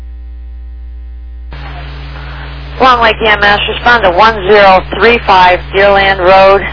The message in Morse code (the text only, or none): none